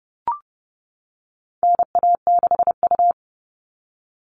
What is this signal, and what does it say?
Morse code, heard as a keyed tone, about 30 wpm, 700 Hz. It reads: NA6U
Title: All Cut Numbers - 4-Digit 30wpm